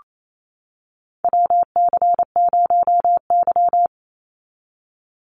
Morse code, heard as a keyed tone, about 28 wpm, 700 Hz. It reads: WC0Y
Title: NA SOTA Chaser Callsigns 28wpm